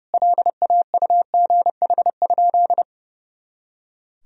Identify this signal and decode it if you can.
Morse code, heard as a keyed tone, about 32 wpm, 700 Hz. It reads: LAUGH?
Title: Sentences from Top 500 Words and Abbreviations 30wpm